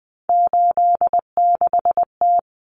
Morse code, heard as a keyed tone, about 20 wpm, 700 Hz. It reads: 86T
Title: Common Cut Numbers - 3-Digit 20wpm